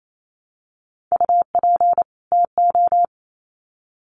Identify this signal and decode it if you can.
Morse code, heard as a keyed tone, about 28 wpm, 700 Hz. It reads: UP TO
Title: Sets of 2 Words 28wpm